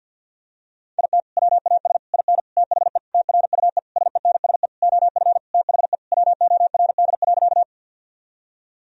Morse code, heard as a keyed tone, about 50 wpm, 700 Hz. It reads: IT WAS IN THE TRUE SENSE OF THE WORD.
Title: Sentences from Top 1000 Words - Review Part 4 - 50wpm